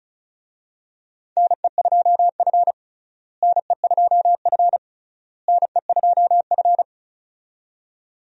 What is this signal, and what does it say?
Morse code, heard as a keyed tone, about 36 wpm, 700 Hz. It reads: NE2F NE2F NE2F
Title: USA Call Signs - 4 Characters - Repeated 3x - 35wpm